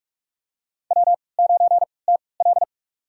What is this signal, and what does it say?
Morse code, heard as a keyed tone, about 44 wpm, spaced as for 25 wpm, 700 Hz. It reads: W9TL
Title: USA Call Signs - 4 Characters - 3x Character Spacing - 45wpm